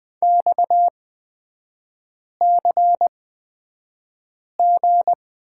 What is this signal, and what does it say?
Morse code, heard as a keyed tone, about 20 wpm, spaced as for 5 wpm, 690 Hz. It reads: XCG
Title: 229 Lesson 37e - Letter x - Three Characters 20wpm